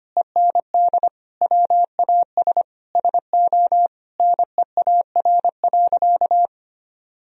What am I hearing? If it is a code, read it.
Morse code, heard as a keyed tone, about 25 wpm, 700 Hz. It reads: END WAS SO NEAR.
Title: Sentences from Top 300 Words and Abbreviations 25wpm